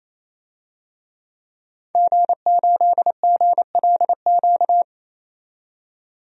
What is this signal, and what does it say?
Morse code, heard as a keyed tone, about 28 wpm, 700 Hz. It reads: G8GLQ
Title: DXCC-100 Call Signs - 5 Characters - 28wpm